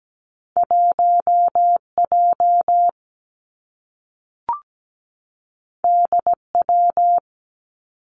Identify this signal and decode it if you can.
Morse code, heard as a keyed tone, about 17 wpm, 700 Hz. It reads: 1 J D W
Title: Two Challenging Letter-Number Combinations 17wpm